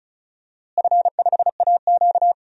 Morse code, heard as a keyed tone, about 36 wpm, 700 Hz. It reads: F5AQ
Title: DXCC-100 Call Signs - 4 Characters - Repeated 2x - 35wpm